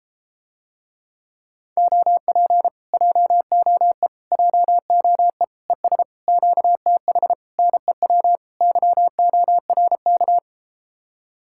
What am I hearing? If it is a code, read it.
Morse code, heard as a keyed tone, about 33 wpm, 700 Hz. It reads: OP JOE JOE ES QTH NEW YORK